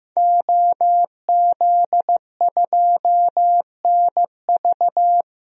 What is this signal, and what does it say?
Morse code, heard as a keyed tone, about 15 wpm, 700 Hz. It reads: OZ2NV
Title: DXCC-100 Call Signs - 5 Characters - 15wpm